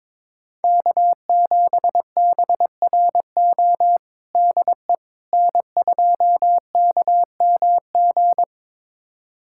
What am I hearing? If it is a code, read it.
Morse code, heard as a keyed tone, about 22 wpm, 700 Hz. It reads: K7BRO DE N2KMG
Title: USA Callsign DE Callsign - 5 Characters - 22wpm